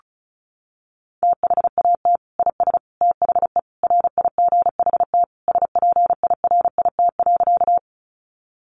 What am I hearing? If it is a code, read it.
Morse code, heard as a keyed tone, about 34 wpm, 700 Hz. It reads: THAT IS THE RIGHT SPIRIT.
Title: Sentences from Top 1400 Words 35wpm